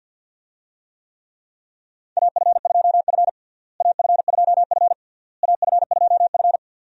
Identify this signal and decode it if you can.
Morse code, heard as a keyed tone, about 52 wpm, 700 Hz. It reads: AF2F AF2F AF2F
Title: USA Call Signs - 4 Characters - Repeated 3x - 50wpm